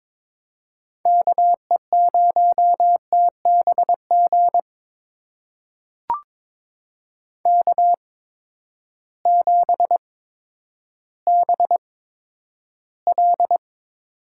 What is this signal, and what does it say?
Morse code, heard as a keyed tone, about 22 wpm, spaced as for 11 wpm, 700 Hz. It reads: KE0TBG K7BL
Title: USA Call Signs - 8x Character Spacing - 22wpm